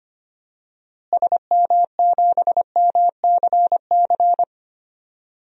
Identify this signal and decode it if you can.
Morse code, heard as a keyed tone, about 25 wpm, 700 Hz. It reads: SM7MCC
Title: DXCC-100 Call Signs - 6 Characters - Repeated 2x - 25wpm